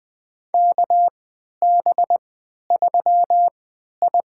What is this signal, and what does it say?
Morse code, heard as a keyed tone, about 20 wpm, spaced as for 11 wpm, 700 Hz. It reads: KB3I
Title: USA Call Signs - 3x Character Spacing - 20wpm